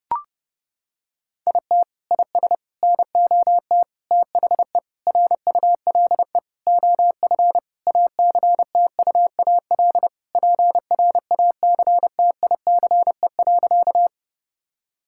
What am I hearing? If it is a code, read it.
Morse code, heard as a keyed tone, about 30 wpm, 700 Hz. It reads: IT IS NOT THE RULE OF ACTUAL PRACTICE.
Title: Sentences from Top 1000 Words - Review Part 5 - 30wpm